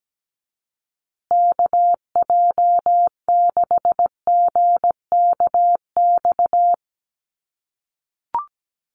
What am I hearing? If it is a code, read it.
Morse code, heard as a keyed tone, about 17 wpm, 700 Hz. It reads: KJ6GKX
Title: USA Call Signs Repeated 3x 17wpm